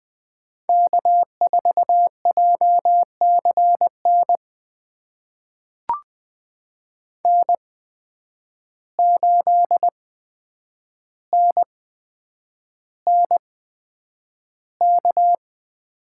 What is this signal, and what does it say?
Morse code, heard as a keyed tone, about 20 wpm, spaced as for 9 wpm, 700 Hz. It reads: K4JCN N8NNK
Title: USA Call Signs - 8x Character Spacing - 20wpm